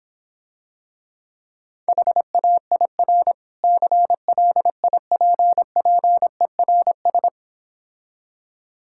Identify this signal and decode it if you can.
Morse code, heard as a keyed tone, about 26 wpm, 700 Hz. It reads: HAIR CLIPPERS